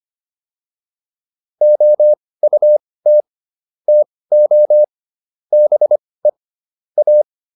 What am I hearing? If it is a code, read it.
Morse code, heard as a keyed tone, about 25 wpm, spaced as for 18 wpm, 600 Hz. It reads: OUT TO BE A